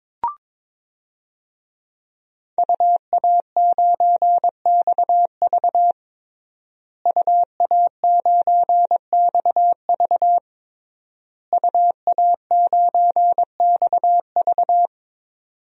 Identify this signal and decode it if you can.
Morse code, heard as a keyed tone, about 22 wpm, 700 Hz. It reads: UA9XV UA9XV UA9XV
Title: DXCC-100 Call Signs - 5 Characters - Repeated 3x - 22wpm